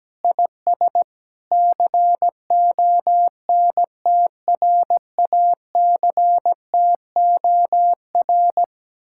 Morse code, heard as a keyed tone, about 17 wpm, 700 Hz. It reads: IS CONTRACTOR
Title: Speed-Racing Common QSO Phrases - 17wpm